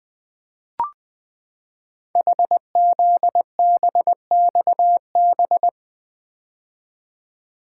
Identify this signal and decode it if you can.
Morse code, heard as a keyed tone, about 20 wpm, 700 Hz. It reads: HZBXB
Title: Code Groups - 5 Letters - 20wpm